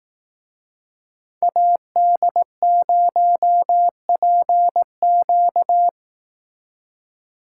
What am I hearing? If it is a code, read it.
Morse code, heard as a keyed tone, about 18 wpm, 700 Hz. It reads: AD0PQ
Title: Speed Racing - USA Call Signs 15wpm